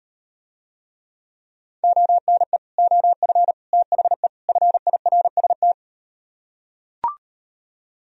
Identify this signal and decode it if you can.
Morse code, heard as a keyed tone, about 38 wpm, 700 Hz. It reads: ONE OF THE FIRST